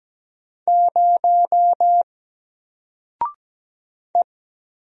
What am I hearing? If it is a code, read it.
Morse code, heard as a keyed tone, about 17 wpm, 700 Hz. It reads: 0 E
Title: Single Letter-Number Alt 17wpm